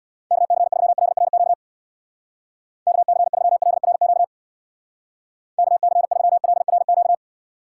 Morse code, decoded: KC3LRQ KC3LRQ KC3LRQ

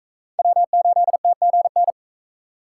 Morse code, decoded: W8TGN

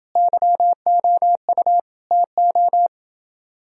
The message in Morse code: YOU TO

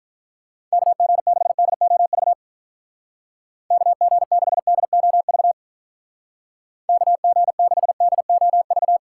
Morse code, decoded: KG6DOV KG6DOV KG6DOV

